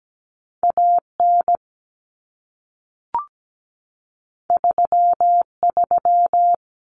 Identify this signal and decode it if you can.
Morse code, heard as a keyed tone, about 17 wpm, 700 Hz. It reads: AN 33
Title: Common Cut Numbers - 2-Digit 17wpm